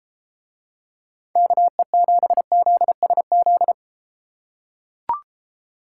Morse code, heard as a keyed tone, about 31 wpm, 700 Hz. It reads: KE7ZSZ